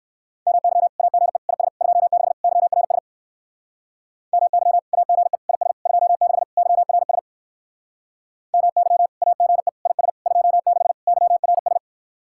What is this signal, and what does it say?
Morse code, heard as a keyed tone, about 65 wpm, 700 Hz. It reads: MY AGE IS 26 YRS MY AGE IS 26 YRS MY AGE IS 26 YRS